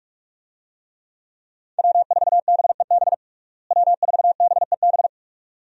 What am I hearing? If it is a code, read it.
Morse code, heard as a keyed tone, about 46 wpm, 700 Hz. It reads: W4BEB W4BEB
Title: USA Call Signs - 5 Characters - Repeated 2x - 45wpm